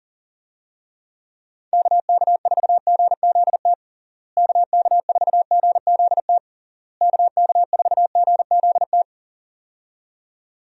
Morse code, heard as a keyed tone, about 40 wpm, 700 Hz. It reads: KK4GZT KK4GZT KK4GZT
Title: USA Call Signs - 6 Characters - Repeated 3x - 40wpm